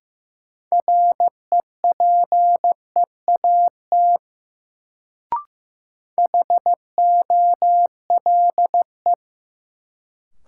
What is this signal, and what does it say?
Morse code, heard as a keyed tone, about 15 wpm, 700 Hz. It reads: REPEAT HOLE
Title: Sentences from Top 1600 Words - New Word Drill - 15wpm